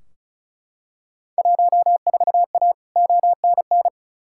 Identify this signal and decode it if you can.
Morse code, heard as a keyed tone, about 36 wpm, 700 Hz. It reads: 14A ONN
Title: Field Day Exchanges Encore - 35wpm